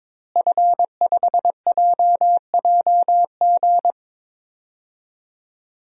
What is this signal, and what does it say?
Morse code, heard as a keyed tone, about 21 wpm, 700 Hz. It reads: F5JJG
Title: DXCC-100 Call Signs - 5 Characters - Repeated 2x - 22wpm